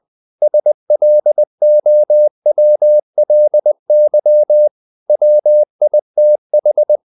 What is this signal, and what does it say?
Morse code, heard as a keyed tone, about 20 wpm, 600 Hz. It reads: SLOWLY WITH
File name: the old man and the sea part 2 20wpm